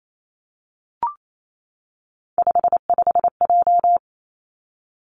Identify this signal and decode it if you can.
Morse code, heard as a keyed tone, about 28 wpm, 700 Hz. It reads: 55J
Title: Three Challenging Letter-Number Combinations 28wpm